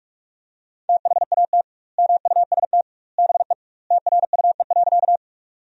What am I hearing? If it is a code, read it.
Morse code, heard as a keyed tone, about 46 wpm, 700 Hz. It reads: THAT MUST BE TRUE.
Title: 2x Word-Spacing - Sentences from Top 400 Words - 45wpm